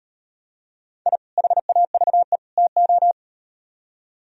Morse code, heard as a keyed tone, about 38 wpm, 700 Hz. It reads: I HAVE TO